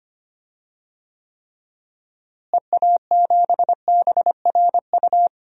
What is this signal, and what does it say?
Morse code, heard as a keyed tone, about 25 wpm, 700 Hz. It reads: EA7BRU
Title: DXCC-100 Call Signs - 6 Characters - Repeated 3x - 25wpm